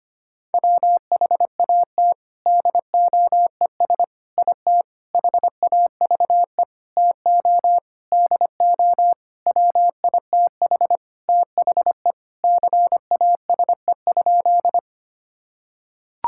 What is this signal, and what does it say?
Morse code, heard as a keyed tone, about 25 wpm, 700 Hz. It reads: WHAT DOES IT HAVE TO DO WITH THE CASE?
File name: practice / 25wpm / Sentences with 4-Word Phrases - 25wpm